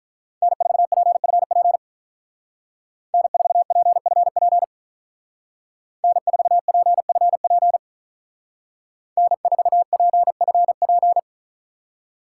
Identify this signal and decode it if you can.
Morse code, heard as a keyed tone, about 42 wpm, 700 Hz. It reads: N4PFP N4PFP N4PFP N4PFP
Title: Speed Racing - USA Call Signs 35wpm